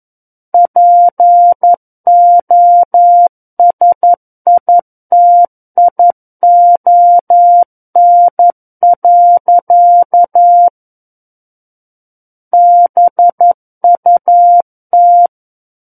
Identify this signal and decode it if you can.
Morse code, heard as a keyed tone, about 11 wpm, 700 Hz. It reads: POSITION. BUT
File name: Book One / 02-Chapter 2 - 11 WPM